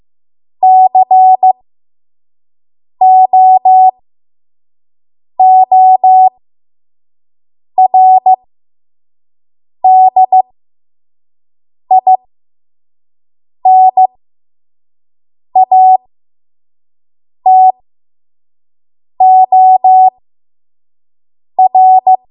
Morse code, heard as a keyed tone, about 15 wpm, spaced as for 5 wpm, 750 Hz. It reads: COORDINATOR
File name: Morse / 05WPM